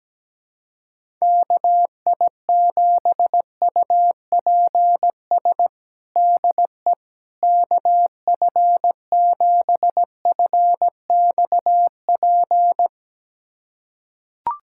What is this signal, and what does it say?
Morse code, heard as a keyed tone, about 17 wpm, 700 Hz. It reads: KI7UPS DE KF7FXP
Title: USA Callsign DE Callsign - 6 Characters - 17wpm